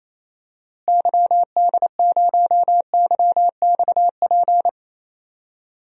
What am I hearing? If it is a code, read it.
Morse code, heard as a keyed tone, about 28 wpm, 700 Hz. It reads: YD0YXP
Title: DXCC-100 Call Signs - 6 Characters - Repeated 2x - 28wpm